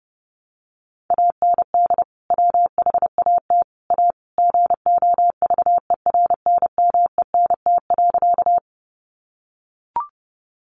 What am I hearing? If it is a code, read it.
Morse code, heard as a keyed tone, about 30 wpm, 700 Hz. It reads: AND WHAT A GOVERNMENT.
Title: Sentences from Top 600 Words 30wpm